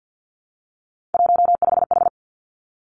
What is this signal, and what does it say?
Morse code, heard as a keyed tone, about 50 wpm, 700 Hz. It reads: 15H
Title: Three Challenging Letter-Number Combinations 50wpm